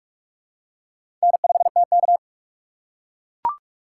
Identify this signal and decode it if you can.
Morse code, heard as a keyed tone, about 42 wpm, 700 Hz. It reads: N5TK
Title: USA Call Signs - 4 Characters - 3x Character Spacing - 45wpm